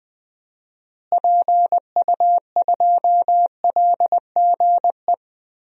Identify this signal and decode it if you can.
Morse code, heard as a keyed tone, about 20 wpm, 700 Hz. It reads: PU2LGE
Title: DXCC-100 Call Signs - 6 Characters - Repeated 2x - 20wpm